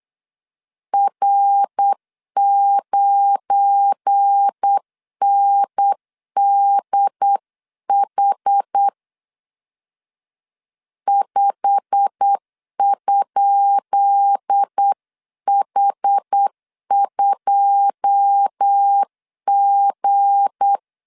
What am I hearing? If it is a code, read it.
Morse code, heard as a keyed tone, about 8 wpm, 800 Hz. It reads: R9NDH 5?H2G
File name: LEZIONI / 10 SIGNS / 10D COMA 8wpm 3749